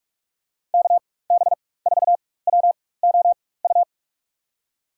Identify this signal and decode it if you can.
Morse code, heard as a keyed tone, about 46 wpm, spaced as for 21 wpm, 700 Hz. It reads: KB4WOU
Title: USA Call Signs - 4x Character Spacing - 45wpm